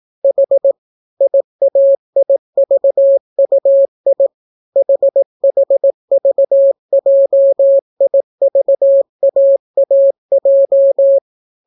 Morse code, heard as a keyed tone, about 18 wpm, 550 Hz. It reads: H IAIVUI HHVJIVAAJ